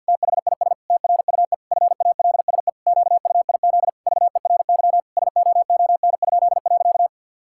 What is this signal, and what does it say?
Morse code, heard as a keyed tone, about 50 wpm, 700 Hz. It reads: THIS TRUE FALSE QUIZ VERY SOON?.
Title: Pangram Challenge v2 - 50wpm